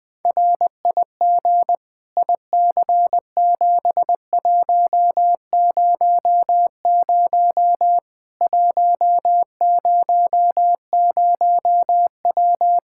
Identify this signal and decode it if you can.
Morse code, heard as a keyed tone, about 21 wpm, 700 Hz. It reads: RIG IC7100 100W
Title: K1USN SST QSO Practice 20wpm